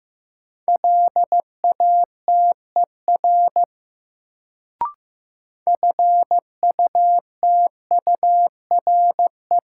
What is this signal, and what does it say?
Morse code, heard as a keyed tone, about 15 wpm, 700 Hz. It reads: LATER FUTURE
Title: Sentences from Top 700 Words - New Word Drill - 15wpm